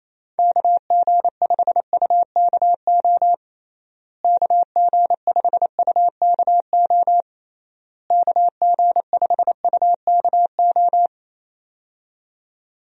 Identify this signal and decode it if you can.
Morse code, heard as a keyed tone, about 28 wpm, 700 Hz. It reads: KG5UKO KG5UKO KG5UKO
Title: USA Call Signs - 6 Characters - Repeated 3x - 28wpm